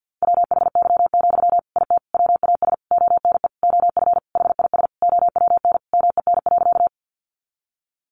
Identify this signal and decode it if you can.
Morse code, heard as a keyed tone, about 50 wpm, 700 Hz. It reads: WHY, IT WAS ONE OF HIS OWN MEN.
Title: Sentences from Top 300 Words 50wpm